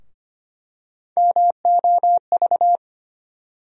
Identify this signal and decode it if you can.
Morse code, heard as a keyed tone, about 25 wpm, 700 Hz. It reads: MOV
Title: USA Call Sign Postfixes - 3 Letters 25wpm